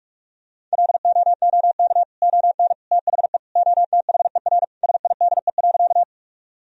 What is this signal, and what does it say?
Morse code, heard as a keyed tone, about 46 wpm, 700 Hz. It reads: LOOK ON THE OTHER SIDE.
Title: Sentences from Top 1000 Words - Review Part 1 - 45wpm